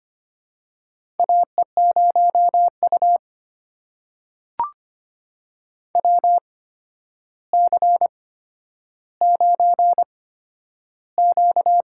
Morse code, heard as a keyed tone, about 26 wpm, spaced as for 11 wpm, 700 Hz. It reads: AE0U WC9Q
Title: USA Call Signs - 4 Characters - 8x Character Spacing - 25wpm